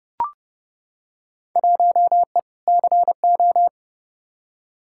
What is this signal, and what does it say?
Morse code, heard as a keyed tone, about 30 wpm, 700 Hz. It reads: 1E CO
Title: Field Day Exchanges - 30wpm